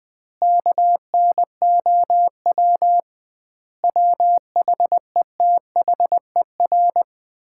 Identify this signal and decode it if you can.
Morse code, heard as a keyed tone, about 20 wpm, 700 Hz. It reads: KNOW WHETHER
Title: 2x Word-Spacing - Sets of 5 Words - 20wpm